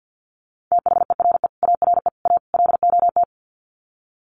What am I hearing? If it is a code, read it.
Morse code, heard as a keyed tone, about 50 wpm, 700 Hz. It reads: THERE ARE A LOT
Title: Sets of 4 Words 50wpm